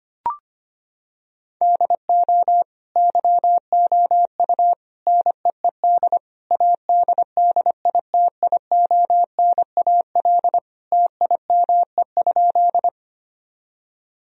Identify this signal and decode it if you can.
Morse code, heard as a keyed tone, about 25 wpm, 700 Hz. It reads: DO YOU NEED ADDITIONAL TIME?
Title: Sentences from Top 1000 Words - Review Part 4 - 25wpm